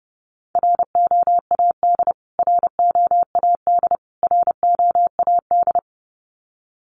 Morse code, heard as a keyed tone, about 30 wpm, 700 Hz. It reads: ROAD ROAD ROAD
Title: Top 500 Words Repeated 30wpm